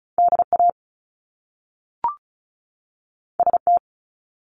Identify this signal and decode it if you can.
Morse code, heard as a keyed tone, about 36 wpm, 700 Hz. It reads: DA ST
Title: Top Two Letter Combinations in Words 35wpm